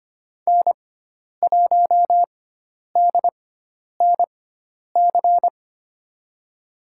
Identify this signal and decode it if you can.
Morse code, heard as a keyed tone, about 25 wpm, spaced as for 10 wpm, 700 Hz. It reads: N1DNC
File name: Lesson 16g - Letter c - Call Signs 25wpm